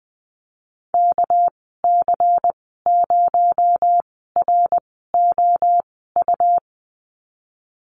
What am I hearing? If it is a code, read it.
Morse code, heard as a keyed tone, about 20 wpm, spaced as for 14 wpm, 700 Hz. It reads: KC0ROU
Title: 238 Lesson 38g - Letter q - Call Signs 20wpm